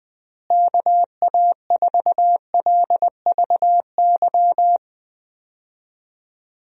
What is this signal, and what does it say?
Morse code, heard as a keyed tone, about 20 wpm, 700 Hz. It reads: KA4LVY
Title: USA Call Signs - 6 Characters - Repeated 2x - 20wpm